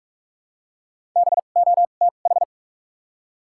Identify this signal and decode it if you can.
Morse code, heard as a keyed tone, about 46 wpm, spaced as for 33 wpm, 700 Hz. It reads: BOTH
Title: 2x Character Spacing - Top 300 Words - 45wpm